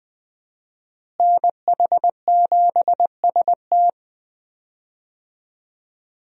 Morse code, heard as a keyed tone, about 20 wpm, 700 Hz. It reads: NH7ST
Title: USA Call Signs - 5 Characters - Repeated 3x - 20wpm